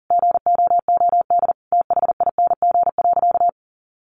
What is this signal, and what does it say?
Morse code, heard as a keyed tone, about 40 wpm, 700 Hz. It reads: GOOD THING.
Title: Sentences from Top 1500 Words 40wpm